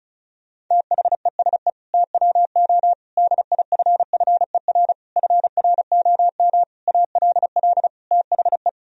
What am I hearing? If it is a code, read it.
Morse code, heard as a keyed tone, about 34 wpm, 700 Hz. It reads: THESE TWO DIFFER FROM ALL THE
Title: Sentences from Top 1000 Words - Review Part 1 - 35wpm